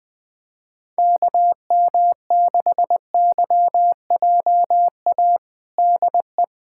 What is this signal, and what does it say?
Morse code, heard as a keyed tone, about 20 wpm, 700 Hz. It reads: KM6YJA DE
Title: USA Callsign DE Callsign - 6 Characters - 20wpm